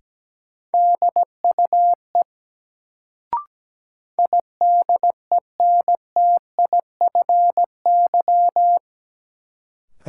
Morse code, heard as a keyed tone, about 17 wpm, 700 Hz. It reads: DUE IDENTIFY
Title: Sentences from Top 900 Words - New Word Drill - 17wpm